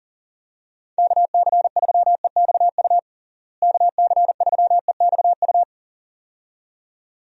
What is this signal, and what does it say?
Morse code, heard as a keyed tone, about 37 wpm, 700 Hz. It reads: KC3EXU KC3EXU
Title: USA Call Signs - 6 Characters - Repeated 2x - 40wpm